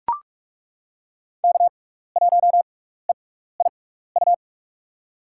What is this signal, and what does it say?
Morse code, heard as a keyed tone, about 43 wpm, spaced as for 15 wpm, 700 Hz. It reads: K1EIU
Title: USA Call Signs - 5 Characters - 6x Character Spacing - 45wpm